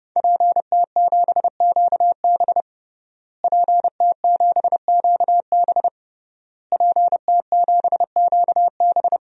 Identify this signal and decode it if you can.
Morse code, heard as a keyed tone, about 30 wpm, 700 Hz. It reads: PT7QB PT7QB PT7QB
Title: DXCC-100 Call Signs - 5 Characters - Repeated 3x - 30wpm